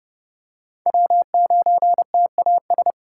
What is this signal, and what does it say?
Morse code, heard as a keyed tone, about 30 wpm, 700 Hz. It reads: W9TAS
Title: USA Call Signs - 4x Character Spacing - 30wpm